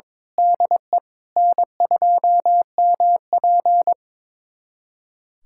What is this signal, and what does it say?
Morse code, heard as a keyed tone, about 22 wpm, 700 Hz. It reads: DE N2MP
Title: USA Callsign DE Callsign 22wpm